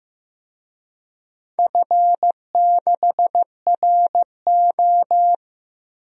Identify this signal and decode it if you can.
Morse code, heard as a keyed tone, about 15 wpm, 700 Hz. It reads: F6RO
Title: DXCC-100 Call Signs - 4 Characters - 15wpm